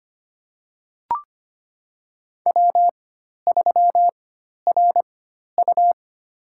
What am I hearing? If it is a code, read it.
Morse code, heard as a keyed tone, about 25 wpm, spaced as for 12 wpm, 700 Hz. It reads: W3RU
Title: USA Call Signs - 4x Character Spacing - 25wpm